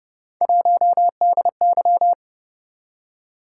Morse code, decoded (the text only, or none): 1DY